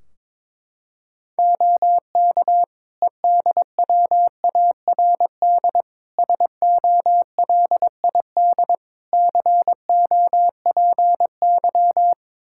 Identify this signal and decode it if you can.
Morse code, heard as a keyed tone, about 22 wpm, 700 Hz. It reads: OK EDWARD SOLID COPY
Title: Practice QSO - Exchange 3 - More Info - 22wpm